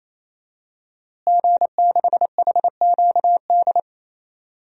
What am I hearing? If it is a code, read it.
Morse code, heard as a keyed tone, about 28 wpm, 700 Hz. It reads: G6HQD
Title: DXCC-100 Call Signs - 5 Characters - Repeated 3x - 28wpm